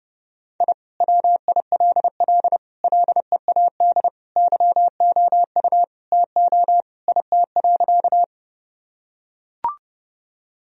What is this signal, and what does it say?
Morse code, heard as a keyed tone, about 30 wpm, 700 Hz. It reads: I WILL LEAD YOU TO IT.
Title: Sentences from Top 1000 Words - Review Part 2 - 30wpm